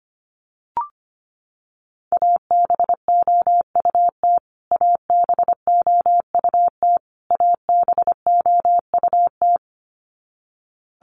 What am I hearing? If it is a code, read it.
Morse code, heard as a keyed tone, about 25 wpm, 700 Hz. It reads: ABOUT ABOUT ABOUT
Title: Top 200 Words Repeated 25wpm